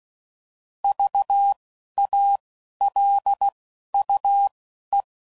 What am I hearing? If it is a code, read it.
Morse code, heard as a keyed tone, about 16 wpm, spaced as for 12 wpm, 800 Hz. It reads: VALUE
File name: CW Code Groups @10WPM / Technical Subjects / XC